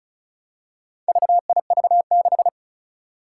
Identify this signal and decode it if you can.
Morse code, heard as a keyed tone, about 36 wpm, 700 Hz. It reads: VIV6
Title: Four Challenging Letter-Number Combinations 35wpm